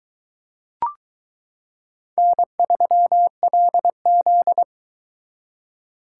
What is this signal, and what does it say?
Morse code, heard as a keyed tone, about 22 wpm, 700 Hz. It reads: N3LZ